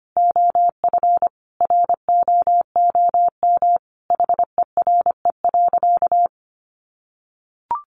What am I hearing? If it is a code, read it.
Morse code, heard as a keyed tone, about 25 wpm, 700 Hz. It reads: OF ROOM HERE.
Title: Sentences from Top 1300 Words 25wpm